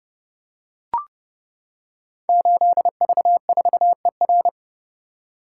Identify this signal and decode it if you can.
Morse code, heard as a keyed tone, about 30 wpm, 700 Hz. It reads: 8V4ER
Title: Code Groups - 5 Letter:Numbers - 30wpm